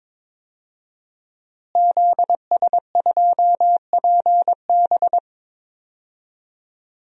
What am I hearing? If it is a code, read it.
Morse code, heard as a keyed tone, about 22 wpm, 700 Hz. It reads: ZS2PB
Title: DXCC-100 Call Signs - 5 Characters - Repeated 2x - 22wpm